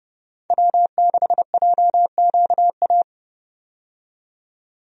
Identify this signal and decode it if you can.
Morse code, heard as a keyed tone, about 30 wpm, 700 Hz. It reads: W6JQA
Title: USA Call Signs - 5 Characters - 30wpm